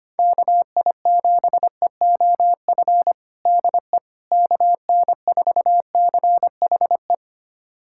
Morse code, heard as a keyed tone, about 25 wpm, 700 Hz. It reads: KI7EOF DE KN4CHE
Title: USA Callsign DE Callsign - 6 Characters - 25wpm